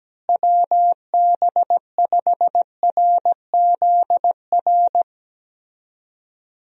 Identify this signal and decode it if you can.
Morse code, heard as a keyed tone, about 17 wpm, 700 Hz. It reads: WB5RZR
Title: USA Call Signs - 6 Characters - Repeated 3x - 17wpm